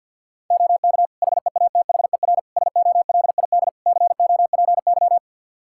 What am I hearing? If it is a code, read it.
Morse code, heard as a keyed tone, about 51 wpm, 700 Hz. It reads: OK HEATHER SOLID COPY
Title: K1USN SST QSO Practice 50wpm